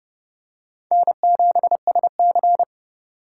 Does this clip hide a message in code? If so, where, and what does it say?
Yes: Morse code from 0.9 to 2.6 s, heard as a keyed tone, about 29 wpm, 700 Hz. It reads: N7SC